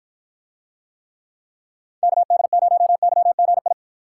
Morse code, heard as a keyed tone, about 47 wpm, 700 Hz. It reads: KD0YGI